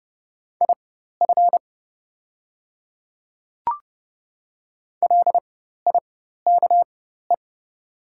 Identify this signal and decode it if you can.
Morse code, heard as a keyed tone, about 30 wpm, spaced as for 14 wpm, 700 Hz. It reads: IF LIKE
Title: 4x Character Spacing - Top 100 Words - 30wpm